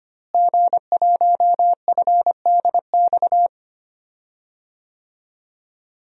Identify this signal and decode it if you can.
Morse code, heard as a keyed tone, about 25 wpm, 700 Hz. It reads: G1FDX
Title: DXCC-100 Call Signs - 5 Characters - Repeated 3x - 25wpm